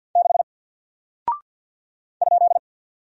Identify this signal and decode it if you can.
Morse code, heard as a keyed tone, about 49 wpm, 700 Hz. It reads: 6 ?
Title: Single Character (Letter:Number:Punctuation:Prosign) Alt 50wpm